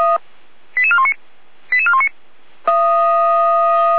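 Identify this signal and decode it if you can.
Five-tone selective call (ZVEI), heard as quick runs of five tones, tones of about 70 ms. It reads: 88419 88419
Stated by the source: ZVEI five-tone call received by SDR, from the audio server of Feuerwehr Orlamünde.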